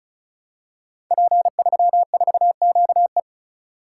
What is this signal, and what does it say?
Morse code, heard as a keyed tone, about 36 wpm, 700 Hz. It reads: P34QE
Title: Code Groups - 5 Letter:Numbers - 35wpm